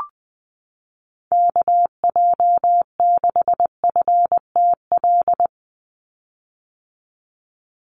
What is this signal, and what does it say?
Morse code, heard as a keyed tone, about 20 wpm, 700 Hz. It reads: KJ6FTL
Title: USA Call Signs 20wpm